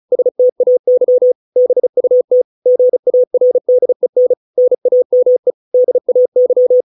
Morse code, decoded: STAY BUT GARDEN NAME DAY

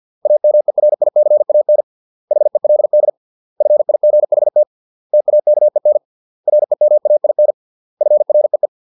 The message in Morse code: AMERICAN HELD FIGHT TAKEN REMAIN FREE

